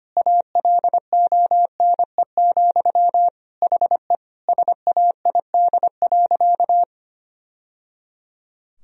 Morse code, heard as a keyed tone, about 25 wpm, 700 Hz. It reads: ALONE, HE SAID.
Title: Sentences from Top 1000 Words - Review Part 3 - 25wpm